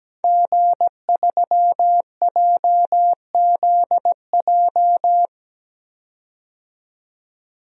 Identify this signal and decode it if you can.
Morse code, heard as a keyed tone, about 17 wpm, 700 Hz. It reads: G3JZJ
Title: DXCC-100 Call Signs - 5 Characters - 17wpm